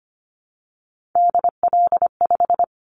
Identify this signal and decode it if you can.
Morse code, heard as a keyed tone, about 25 wpm, 700 Hz. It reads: DL5
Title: Callsign Prefixes for Top 50 DXCC - 25wpm